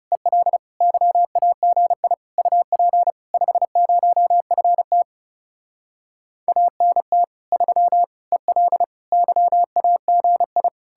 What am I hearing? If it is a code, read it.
Morse code, heard as a keyed tone, about 40 wpm, 700 Hz. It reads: EL YAGI UP 50FT ANT 3 EL YAGI